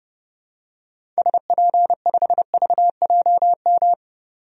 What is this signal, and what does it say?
Morse code, heard as a keyed tone, about 29 wpm, 690 Hz. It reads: SP5VJM
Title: DXCC-100 Call Signs - 6 Characters - 30wpm